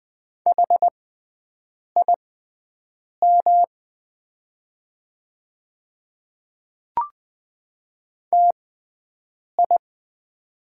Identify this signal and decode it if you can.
Morse code, heard as a keyed tone, about 20 wpm, spaced as for 7 wpm, 700 Hz. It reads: HIM TI
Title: 6x Character Spacing - Top 100 Words - 20wpm